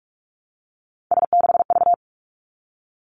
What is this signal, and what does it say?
Morse code, heard as a keyed tone, about 46 wpm, 700 Hz. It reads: S6V
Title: Three Challenging Letter-Number Combinations 45wpm